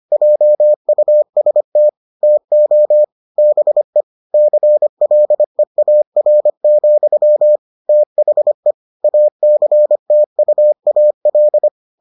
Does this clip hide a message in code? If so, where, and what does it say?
Yes: Morse code from 0.1 to 11.7 s, heard as a keyed tone, about 25 wpm, 600 Hz. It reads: JUST TO BE CLEAR, THE ACTUAL